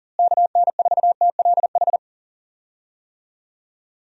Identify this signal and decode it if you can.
Morse code, heard as a keyed tone, about 39 wpm, 700 Hz. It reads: KN4TLH
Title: USA Call Signs - 6 Characters - Repeated 3x - 40wpm